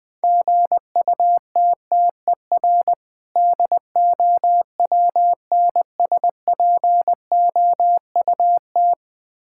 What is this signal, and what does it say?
Morse code, heard as a keyed tone, about 21 wpm, 700 Hz. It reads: GUTTER DOWNSPOUT
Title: Difficult Words - Speed-Racing with Hints - 20wpm